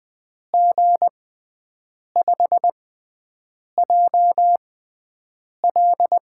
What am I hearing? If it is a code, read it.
Morse code, heard as a keyed tone, about 20 wpm, spaced as for 7 wpm, 700 Hz. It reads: G5JL
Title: USA Call Signs - 6 Characters - 6x Character Spacing - 20wpm